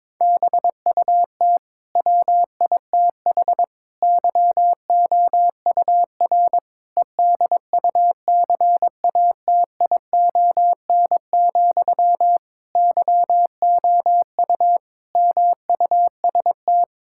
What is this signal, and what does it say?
Morse code, heard as a keyed tone, about 21 wpm, 700 Hz. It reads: BUT WITH YOUR EDUCATION, YOU MUST
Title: Sentences from Top 1000 Words - Review Part 6 - 22wpm